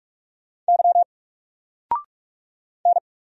Morse code, heard as a keyed tone, about 44 wpm, 700 Hz. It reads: Y N
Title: Single Character (Letter:Number:Punctuation:Prosign) 45wpm